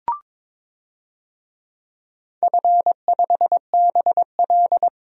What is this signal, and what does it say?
Morse code, heard as a keyed tone, about 22 wpm, 700 Hz. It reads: F5BL
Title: DXCC-100 Call Signs - 4 Characters - 22wpm